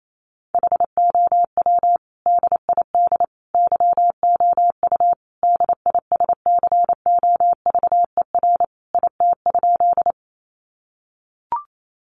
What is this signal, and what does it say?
Morse code, heard as a keyed tone, about 28 wpm, 700 Hz. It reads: HOW DID YOU DISCOVER IT?
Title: Sentences from Top 1100 Words 28wpm